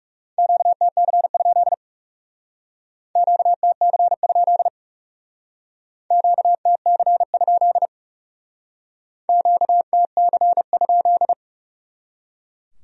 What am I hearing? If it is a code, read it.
Morse code, heard as a keyed tone, about 38 wpm, 700 Hz. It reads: QTC? QTC? QTC? QTC?